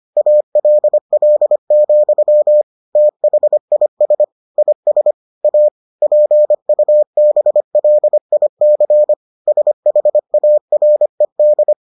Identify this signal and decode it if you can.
Morse code, heard as a keyed tone, about 25 wpm, 600 Hz. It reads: ALL, THIS IS A PUBLIC SHARED